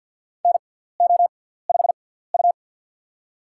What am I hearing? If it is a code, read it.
Morse code, heard as a keyed tone, about 46 wpm, spaced as for 17 wpm, 700 Hz. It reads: NO5U
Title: USA Call Signs - 4 Characters - 6x Character Spacing - 50wpm